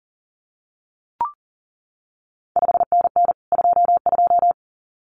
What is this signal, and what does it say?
Morse code, heard as a keyed tone, about 38 wpm, 700 Hz. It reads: 5NN 22
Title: CQ WW DX Contest Exchange Encore - 40wpm